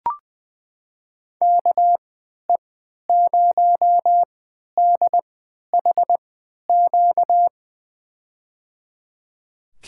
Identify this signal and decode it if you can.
Morse code, heard as a keyed tone, about 20 wpm, spaced as for 11 wpm, 700 Hz. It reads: KE0DHQ
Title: USA Call Signs - 6 Characters - 3x Character Spacing - 20wpm